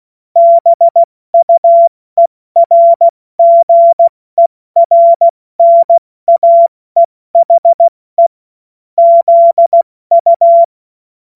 Morse code, heard as a keyed tone, about 16 wpm, 680 Hz. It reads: BUERGERNAEHE ZU
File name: Morsekurs / Witze / Witz 03 Juni Mai 2025 70 F80